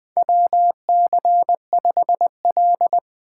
Morse code, heard as a keyed tone, about 20 wpm, 700 Hz. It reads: WC5L
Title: USA Call Signs - 4 Characters - 3x Character Spacing - 20wpm